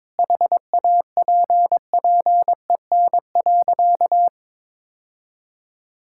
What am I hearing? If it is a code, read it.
Morse code, heard as a keyed tone, about 22 wpm, 700 Hz. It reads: HAPPEN.
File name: practice / 22wpm / Sentences with 3-Word Phrases - 22wpm